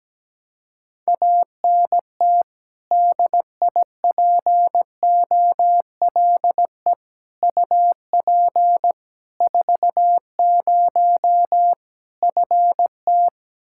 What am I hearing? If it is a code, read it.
Morse code, heard as a keyed tone, about 17 wpm, 700 Hz. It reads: ANT DIPOLE UP 40 FT
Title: Practice QSO - Exchange 2 - Info - 17wpm